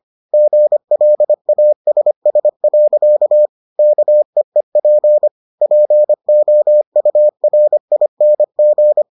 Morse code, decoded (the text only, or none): GLASS. KEEP POURING